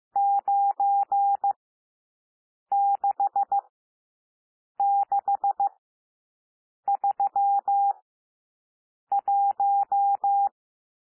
Morse code, heard as a keyed tone, about 15 wpm, spaced as for 6 wpm, 800 Hz. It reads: 96631
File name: cwmp3 / N1 SOUND NUMBERS 15 wpm